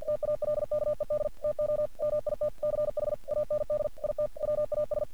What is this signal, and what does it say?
Morse code, heard as a keyed tone, about 45 wpm, 600 Hz. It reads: TALKED TO MITCH AND IT WAS
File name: archived / CRSnet